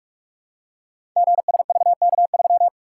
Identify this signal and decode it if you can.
Morse code, heard as a keyed tone, about 43 wpm, 700 Hz. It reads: GSVK3
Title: Code Groups - 5 Letter:Numbers - 45wpm